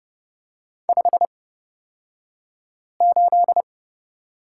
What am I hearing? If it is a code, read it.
Morse code, heard as a keyed tone, about 30 wpm, spaced as for 5 wpm, 700 Hz. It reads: 58
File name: Lesson 38d - Letter q - Two Characters 30wpm